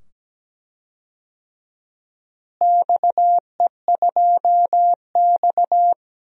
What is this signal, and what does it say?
Morse code, heard as a keyed tone, about 17 wpm, 700 Hz. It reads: XE2X
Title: DXCC-100 Call Signs - 4 Characters - Repeated 2x - 17wpm